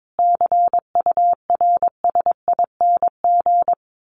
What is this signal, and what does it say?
Morse code, heard as a keyed tone, about 22 wpm, 700 Hz. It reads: CURSING
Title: A Sound Of Thunder 22wpm